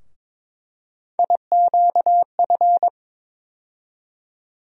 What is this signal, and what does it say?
Morse code, heard as a keyed tone, about 22 wpm, 700 Hz. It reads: IQF
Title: USA Call Sign Postfixes - 3 Letters 22wpm